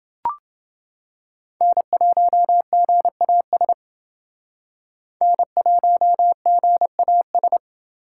Morse code, heard as a keyed tone, about 28 wpm, 700 Hz. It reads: N1GAS N1GAS